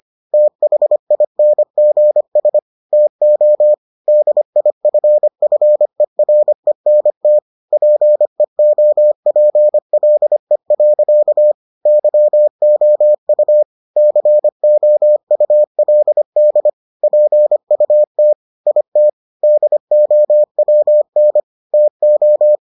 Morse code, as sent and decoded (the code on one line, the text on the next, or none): - .... .. -. --. ... / - --- / -.. .. ..-. ..-. .-. -. - / .--. --- .--. .-.. .-.-.- / -.-- --- ..- / -.-. --- ..- .-.. -.. / .--. ..- - / .. - / -.. --- .-- -. / - ---
THINGS TO DIFFERENT PEOPLE. YOU COULD PUT IT DOWN TO